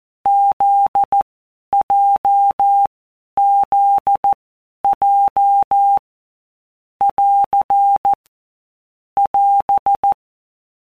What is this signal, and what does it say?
Morse code, heard as a keyed tone, about 14 wpm, 800 Hz. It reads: Z J Z J <AR> <AS>
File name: Basic Letter Groups By WPM's / Groups@13 / JZ17@13